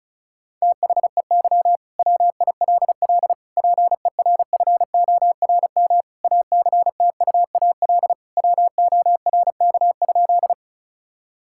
Morse code, heard as a keyed tone, about 36 wpm, 700 Hz. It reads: THEY WILL PERFORM ACTUAL WORK?
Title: Sentences from Top 1000 Words - Review Part 5 - 35wpm